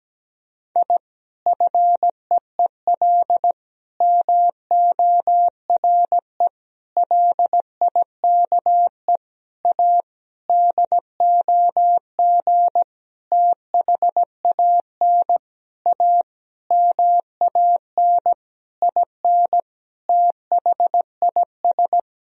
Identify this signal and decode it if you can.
Morse code, heard as a keyed tone, about 17 wpm, 700 Hz. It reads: I FEEL MORE LIKE A DOG THAN A MAN IN THIS
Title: Sentences from Top 1000 Words - Review Part 2 - 17wpm